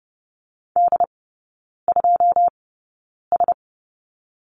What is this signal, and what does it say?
Morse code, heard as a keyed tone, about 30 wpm, spaced as for 9 wpm, 700 Hz. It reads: D2S
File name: course / 30:10wpm / 087 Lesson 15e - Number 5 - Three Characters 30wpm